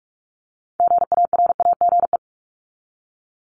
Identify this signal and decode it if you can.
Morse code, heard as a keyed tone, about 44 wpm, 700 Hz. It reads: GARAGE
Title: Top 1501-1750 Words 45wpm